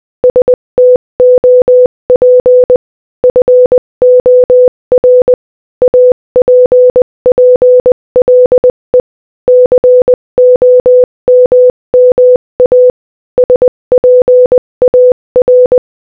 Morse code, heard as a keyed tone, about 20 wpm, 500 Hz. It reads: STOP FOR APPLE COMMA SPAR